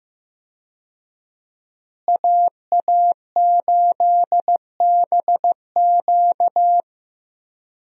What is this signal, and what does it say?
Morse code, heard as a keyed tone, about 15 wpm, 700 Hz. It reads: AA8BQ
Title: USA Call Signs - 5 Characters - Repeated 3x - 15wpm